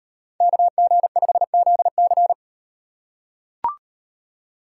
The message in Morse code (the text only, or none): KG5ZC